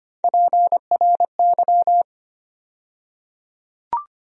Morse code, heard as a keyed tone, about 25 wpm, 700 Hz. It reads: PRY
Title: USA Call Sign Postfixes - 3 Letters 25wpm